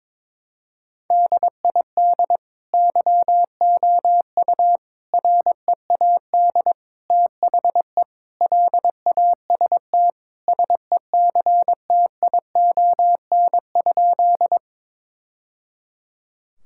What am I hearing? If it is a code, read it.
Morse code, heard as a keyed tone, about 22 wpm, 700 Hz. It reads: DID YOU READ THE LAST SECTION?
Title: Sentences from Top 1000 Words - Review Part 5 - 22wpm